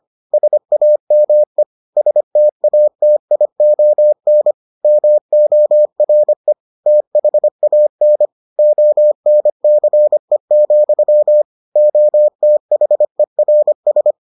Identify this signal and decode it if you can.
Morse code, heard as a keyed tone, about 25 wpm, 600 Hz. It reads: SAME STATION MORE THAN ONCE, OTHERS